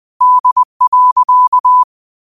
Morse code, heard as a keyed tone, about 20 wpm, 1000 Hz. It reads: D.
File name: qotdfiles / 20wpm